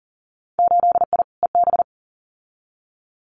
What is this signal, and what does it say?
Morse code, heard as a keyed tone, about 42 wpm, 700 Hz. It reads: 8I EB
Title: Winter Field Day Exchanges Encore 40wpm